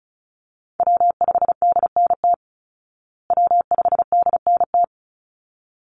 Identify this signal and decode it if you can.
Morse code, heard as a keyed tone, about 34 wpm, 700 Hz. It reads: W5DNT W5DNT
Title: Call Signs Repeated 3x - 35wpm